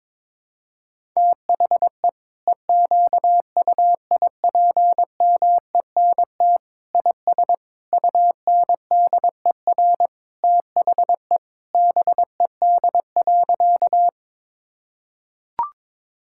Morse code, heard as a keyed tone, about 22 wpm, 700 Hz. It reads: THE EQUIPMENT IS UNDER THE BED.
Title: Sentences from Top 1000 Words - Review Part 5 - 22wpm